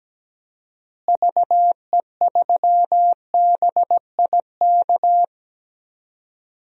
Morse code, heard as a keyed tone, about 17 wpm, 700 Hz. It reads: VE3BIK
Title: DXCC-100 Call Signs - 6 Characters - Repeated 3x - 17wpm